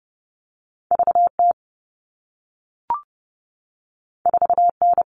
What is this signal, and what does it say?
Morse code, heard as a keyed tone, about 30 wpm, 700 Hz. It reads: VT 4N
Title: All Cut Numbers - 2-Digit 30wpm